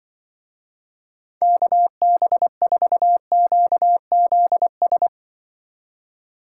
Morse code, heard as a keyed tone, about 26 wpm, 700 Hz. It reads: KB4QZS